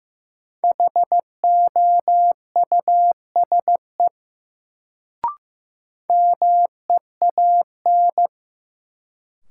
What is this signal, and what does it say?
Morse code, heard as a keyed tone, about 15 wpm, 700 Hz. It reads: HOUSE MEAN
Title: Sentences from Top 200 Words - New Word Drill - 15wpm